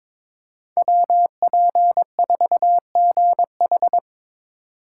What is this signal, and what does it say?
Morse code, heard as a keyed tone, about 22 wpm, 700 Hz. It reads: WP4GH
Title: USA Call Signs - 5 Characters - 4x Character Spacing - 22wpm